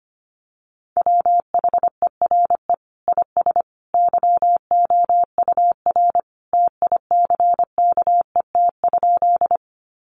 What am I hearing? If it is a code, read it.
Morse code, heard as a keyed tone, about 25 wpm, 700 Hz. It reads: WHERE IS YOUR TICKET?
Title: Sentences from Top 1400 Words 25wpm